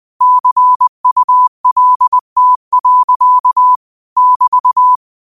Morse code, heard as a keyed tone, about 20 wpm, 1000 Hz. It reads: CULT. <BT>
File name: qotdfiles / 20wpm